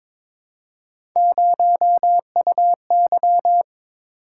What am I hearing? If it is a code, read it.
Morse code, heard as a keyed tone, about 22 wpm, 700 Hz. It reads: 0UY
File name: practice / 22wpm / Three Letter-Number Combinations - 22wpm